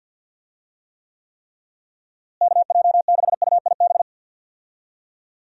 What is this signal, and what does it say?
Morse code, heard as a keyed tone, about 51 wpm, 700 Hz. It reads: KJ6UIB
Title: USA Call Signs - 6 Characters - Repeated 3x - 50wpm